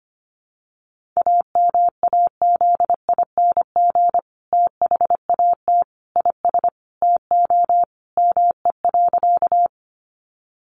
Morse code, heard as a keyed tone, about 25 wpm, 700 Hz. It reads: AMAZING THAT IS TO ME.
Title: Sentences from Top 1500 Words 25wpm